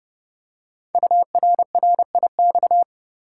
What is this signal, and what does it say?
Morse code, heard as a keyed tone, about 29 wpm, 700 Hz. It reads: URRIX
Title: Code Groups - 5 Letters - 30wpm